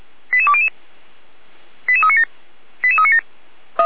Five-tone selective call (ZVEI), heard as quick runs of five tones, tones of about 70 ms. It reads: 88200 88397 88397